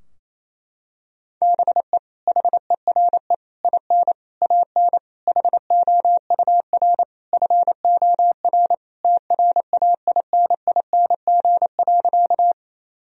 Morse code, heard as a keyed tone, about 28 wpm, 700 Hz. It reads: BE HERE IN AN HOUR FOR TRAINING.
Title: Sentences from Top 1000 Words - Review Part 4 - 28wpm